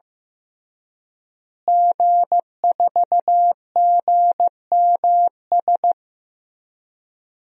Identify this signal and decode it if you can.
Morse code, heard as a keyed tone, about 15 wpm, 700 Hz. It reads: G4GMS
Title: DXCC-100 Call Signs - 5 Characters - Repeated 3x - 15wpm